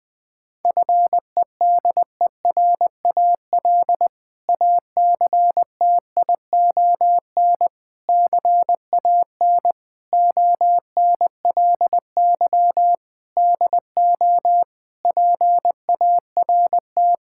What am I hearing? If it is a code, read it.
Morse code, heard as a keyed tone, about 20 wpm, 700 Hz. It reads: FEDERAL ACTION CAN ONLY DO PART
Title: Sentences from Top 1000 Words - Review Part 5 - 20wpm